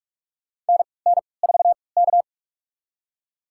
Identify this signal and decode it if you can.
Morse code, heard as a keyed tone, about 46 wpm, spaced as for 26 wpm, 690 Hz. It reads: NN4K
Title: USA Call Signs - 4 Characters - 3x Character Spacing - 45wpm